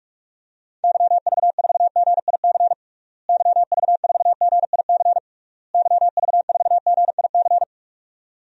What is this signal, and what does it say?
Morse code, heard as a keyed tone, about 46 wpm, 700 Hz. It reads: YV4GIC YV4GIC YV4GIC
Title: DXCC-100 Call Signs - 6 Characters - Repeated 3x - 45wpm